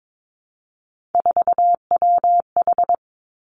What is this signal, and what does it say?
Morse code, heard as a keyed tone, about 22 wpm, 700 Hz. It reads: 4WH
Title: Three Challenging Letter-Number Combinations 22wpm